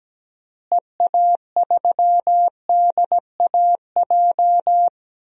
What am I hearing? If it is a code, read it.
Morse code, heard as a keyed tone, about 17 wpm, 700 Hz. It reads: EA3DAJ
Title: DXCC-100 Call Signs - 6 Characters - Repeated 3x - 17wpm